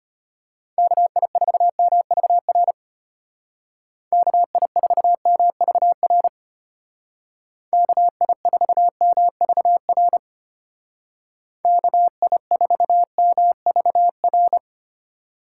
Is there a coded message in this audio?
Yes: Morse code from 0.8 to 14.6 s, heard as a keyed tone, about 31 wpm, 700 Hz. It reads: KI4MVR KI4MVR KI4MVR KI4MVR